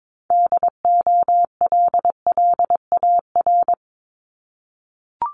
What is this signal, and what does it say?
Morse code, heard as a keyed tone, about 22 wpm, 700 Hz. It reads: DOLLAR
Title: Multi-letter Phonemes 22wpm